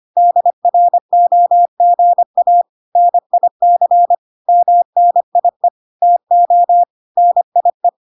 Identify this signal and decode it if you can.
Morse code, heard as a keyed tone, about 25 wpm, 700 Hz. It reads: DROGA NIC MNIE TO NIE
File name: recordings / 25wpm0000